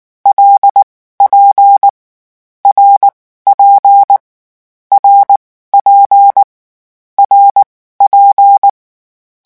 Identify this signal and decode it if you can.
Morse code, heard as a keyed tone, about 19 wpm, spaced as for 14 wpm, 800 Hz. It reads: LP RP RP RP
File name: morse mp3 / lecon6-vitesse 18